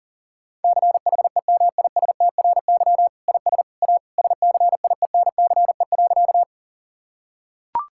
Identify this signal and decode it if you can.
Morse code, heard as a keyed tone, about 37 wpm, 700 Hz. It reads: CHEMISTRY IS A SCIENCE.
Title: Sentences from Top 1000 Words - Review Part 6 - 40wpm